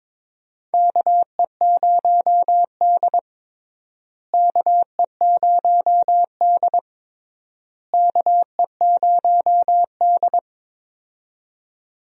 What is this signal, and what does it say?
Morse code, heard as a keyed tone, about 22 wpm, 700 Hz. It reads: KE0D KE0D KE0D
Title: USA Call Signs - 4 Characters - Repeated 3x - 22wpm